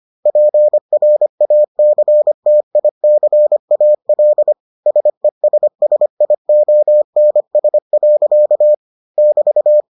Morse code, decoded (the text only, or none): PRACTICAL SESSIONS. <BT>